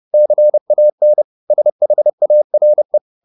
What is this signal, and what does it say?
Morse code, heard as a keyed tone, about 30 wpm, 600 Hz. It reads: CAN SHARE
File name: the old man and the sea / the old man and the sea 1 30wpm